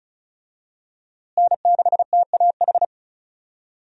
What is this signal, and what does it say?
Morse code, heard as a keyed tone, about 34 wpm, 700 Hz. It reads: N6TAH
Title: USA Call Signs - 5 Characters - Repeated 2x - 35wpm